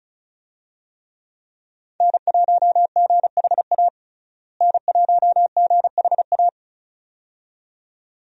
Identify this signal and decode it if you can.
Morse code, heard as a keyed tone, about 35 wpm, 700 Hz. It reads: N1GHA N1GHA
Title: USA Call Signs - 5 Characters - Repeated 2x - 35wpm